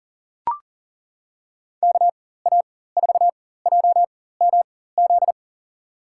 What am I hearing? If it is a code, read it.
Morse code, heard as a keyed tone, about 40 wpm, spaced as for 19 wpm, 700 Hz. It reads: KA4JMZ
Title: USA Call Signs - 6 Characters - 4x Character Spacing - 40wpm